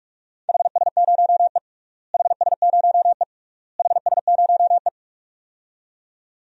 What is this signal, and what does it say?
Morse code, heard as a keyed tone, about 46 wpm, 700 Hz. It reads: HS0E HS0E HS0E
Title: DXCC-100 Call Signs - 4 Characters - Repeated 3x - 45wpm